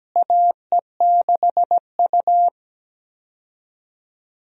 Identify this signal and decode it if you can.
Morse code, heard as a keyed tone, about 17 wpm, 700 Hz. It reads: AE6U
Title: USA Call Signs - 4 Characters - Repeated 2x - 17wpm